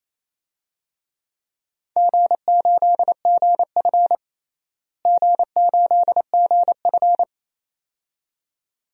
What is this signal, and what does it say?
Morse code, heard as a keyed tone, about 28 wpm, 700 Hz. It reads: G8GF G8GF
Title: DXCC-100 Call Signs - 4 Characters - Repeated 2x - 28wpm